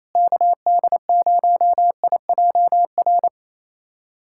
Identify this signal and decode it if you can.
Morse code, heard as a keyed tone, about 27 wpm, 700 Hz. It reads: KD0IJR
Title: USA Callsign DE Callsign - 6 Characters - 28wpm